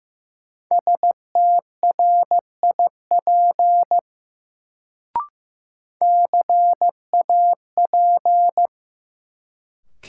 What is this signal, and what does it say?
Morse code, heard as a keyed tone, about 15 wpm, 700 Hz. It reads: STRIP CAP
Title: Sentences from Top 1600 Words - New Word Drill - 15wpm